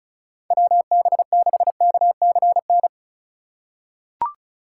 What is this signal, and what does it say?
Morse code, heard as a keyed tone, about 34 wpm, 700 Hz. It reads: WB6KCN
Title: USA Call Signs - 6 Characters - 2x Character Spacing - 35wpm